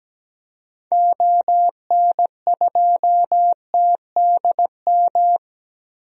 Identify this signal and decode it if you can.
Morse code, heard as a keyed tone, about 17 wpm, 700 Hz. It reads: ON2TDM
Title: DXCC-100 Call Signs - 6 Characters - Repeated 2x - 17wpm